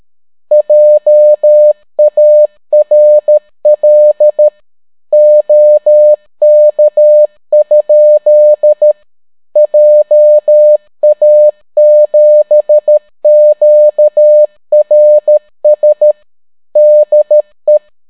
Morse code, heard as a keyed tone, about 13 wpm, 600 Hz. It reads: JARL OK? JA7QRS DE